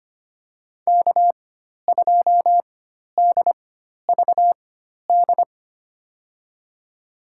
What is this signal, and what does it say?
Morse code, heard as a keyed tone, about 25 wpm, spaced as for 12 wpm, 700 Hz. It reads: K2DVD
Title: USA Call Signs - 5 Characters - 4x Character Spacing - 25wpm